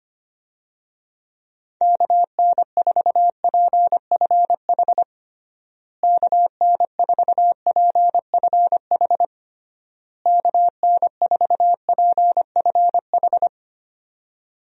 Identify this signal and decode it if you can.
Morse code, heard as a keyed tone, about 25 wpm, 700 Hz. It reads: KN4PFH KN4PFH KN4PFH
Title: USA Call Signs - 6 Characters - Repeated 3x - 25wpm